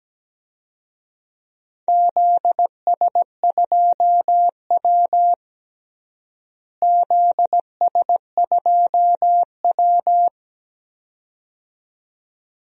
Morse code, heard as a keyed tone, about 17 wpm, 700 Hz. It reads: ZS2W ZS2W
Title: DXCC-100 Call Signs - 4 Characters - Repeated 2x - 17wpm